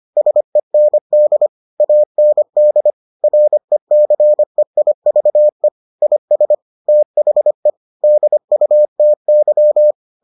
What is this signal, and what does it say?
Morse code, heard as a keyed tone, about 25 wpm, 600 Hz. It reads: SEND AND RECEIVE IS THE DUTY